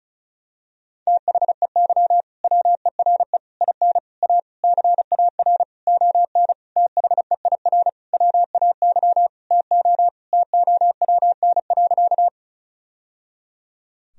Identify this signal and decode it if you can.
Morse code, heard as a keyed tone, about 36 wpm, 700 Hz. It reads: THEY WERE IN A CAR ON THEIR WAY TO TOWN.
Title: Sentences from Top 1000 Words - Review Part 1 - 35wpm